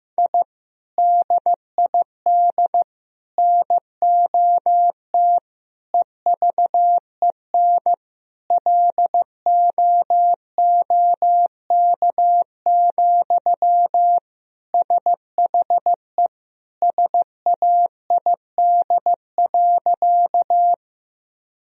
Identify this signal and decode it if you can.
Morse code, heard as a keyed tone, about 15 wpm, 700 Hz. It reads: I DID NOT EVEN LOOK, SHE SAID.
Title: Sentences from Top 1000 Words - Review Part 1 - 15wpm